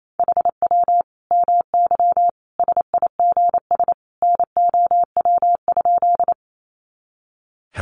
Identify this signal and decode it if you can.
Morse code, heard as a keyed tone, about 28 wpm, 700 Hz. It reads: HW MY SIGS NOW?
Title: Common QSO Phrases 28wpm